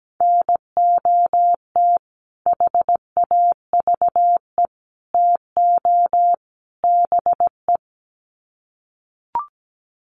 Morse code, heard as a keyed tone, about 17 wpm, 700 Hz. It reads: NOT HAVE TO BE
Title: Sets of 4 Words 17wpm